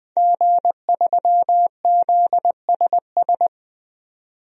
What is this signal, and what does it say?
Morse code, heard as a keyed tone, about 21 wpm, 700 Hz. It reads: G3ZSS
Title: DXCC-100 Call Signs - 5 Characters - 20wpm